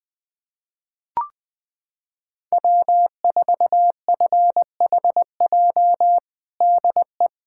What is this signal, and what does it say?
Morse code, heard as a keyed tone, about 20 wpm, 700 Hz. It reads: W4FHJ DE
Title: USA Callsign DE Callsign - 5 Characters - 20wpm